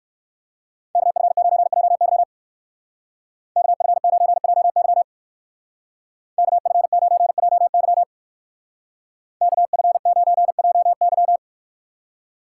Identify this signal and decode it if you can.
Morse code, heard as a keyed tone, about 54 wpm, 700 Hz. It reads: KF9JY KF9JY KF9JY KF9JY
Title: Speed Racing - USA 2x2 Call Signs 45wpm